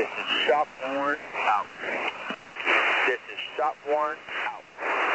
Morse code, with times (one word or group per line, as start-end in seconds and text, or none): none